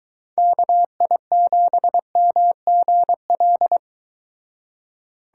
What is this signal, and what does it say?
Morse code, heard as a keyed tone, about 23 wpm, 700 Hz. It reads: KI7MGL